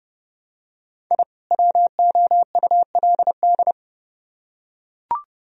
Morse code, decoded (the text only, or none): I WOULD